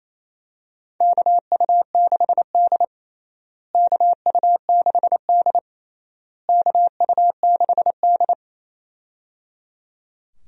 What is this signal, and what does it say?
Morse code, heard as a keyed tone, about 28 wpm, 700 Hz. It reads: KU6D KU6D KU6D
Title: USA Call Signs - 4 Characters - Repeated 3x - 28wpm